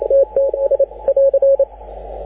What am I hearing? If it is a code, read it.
Morse code, heard as a keyed tone, about 28 wpm, 550 Hz, with radio noise. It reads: TTS <AR>
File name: AUDIO2019 / HG6N CW SSB MIX